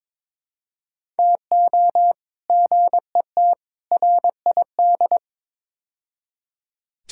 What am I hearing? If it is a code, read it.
Morse code, heard as a keyed tone, about 23 wpm, 700 Hz. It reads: TO GET RID
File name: practice / 22wpm / Build A Phrase - 5-Word Phrases - 22wpm